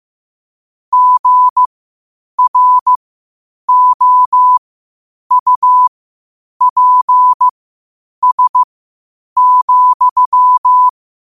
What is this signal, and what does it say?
Morse code, heard as a keyed tone, about 15 wpm, spaced as for 8 wpm, 1000 Hz. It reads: GROUPS,
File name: qotdfiles / 8wpm